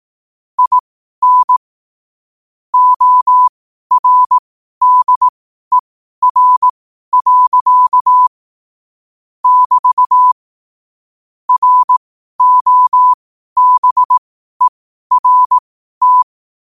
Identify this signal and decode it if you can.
Morse code, heard as a keyed tone, about 18 wpm, spaced as for 12 wpm, 1000 Hz. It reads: IN ORDER. <BT> ROBERT
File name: qotdfiles / 12wpm